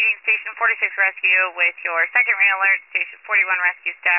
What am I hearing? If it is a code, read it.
Emergency Alert System (SAME) tones, heard as harsh buzzing bursts, with speech. It is heard from 0.6 to 3.7 s.